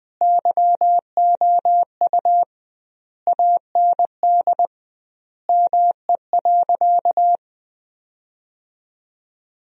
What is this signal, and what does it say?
Morse code, heard as a keyed tone, about 21 wpm, 700 Hz. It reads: YOU AND ME.
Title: 2x Word-Spacing - Sentences from Top 500 Words - 20wpm